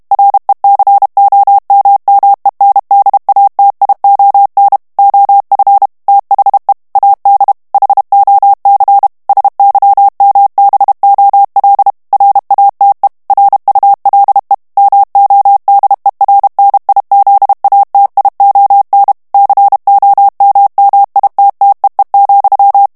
Morse code, heard as a keyed tone, about 32 wpm, 800 Hz. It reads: RECOMMENDATION OF THE AD HOC SYMBOL RATE RULE MODERNIZATION COMMITTEE,
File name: ANGLAIS texte clair 3 vitesse 30 mots minute